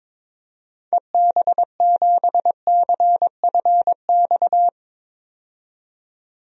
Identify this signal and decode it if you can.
Morse code, heard as a keyed tone, about 22 wpm, 700 Hz. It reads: EB7CFX
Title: DXCC-100 Call Signs - 6 Characters - 22wpm